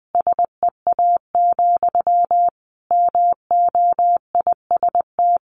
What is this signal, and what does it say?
Morse code, heard as a keyed tone, about 20 wpm, 700 Hz. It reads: SEA, MOIST
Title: A Sound Of Thunder 20wpm